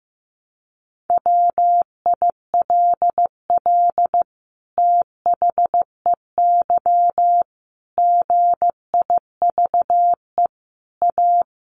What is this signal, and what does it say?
Morse code, heard as a keyed tone, about 15 wpm, 700 Hz. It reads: WILL THEY GIVE A
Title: Sentences from Top 1400 Words 15wpm